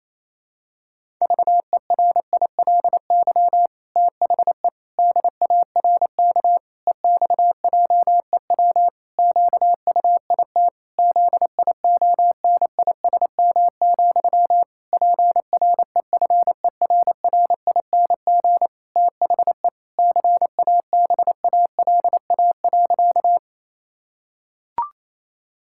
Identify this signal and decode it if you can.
Morse code, heard as a keyed tone, about 28 wpm, 700 Hz. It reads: VERILY THE DARK EXJEW QUIT ZIONISM, PREFERRING THE CABALA.
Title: Pangram Challenge v2 - 28wpm